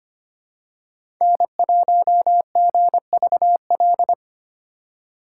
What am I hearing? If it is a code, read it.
Morse code, heard as a keyed tone, about 25 wpm, 700 Hz. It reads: N1GVL
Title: USA Call Signs - 5 Characters - 4x Character Spacing - 25wpm